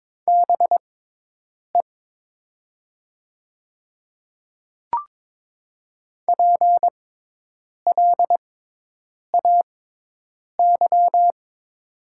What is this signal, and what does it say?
Morse code, heard as a keyed tone, about 22 wpm, spaced as for 8 wpm, 700 Hz. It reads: BE PLAY
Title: 6x Character Spacing - Top 300 Words - 22wpm